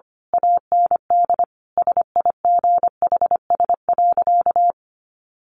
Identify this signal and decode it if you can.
Morse code, heard as a keyed tone, about 25 wpm, 700 Hz. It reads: AND SIGHS.
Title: A Sound Of Thunder 25wpm